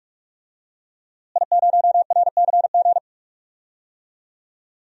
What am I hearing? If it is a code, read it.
Morse code, heard as a keyed tone, about 46 wpm, 700 Hz. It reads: I0RCG
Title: DXCC-100 Call Signs - 5 Characters - Repeated 2x - 45wpm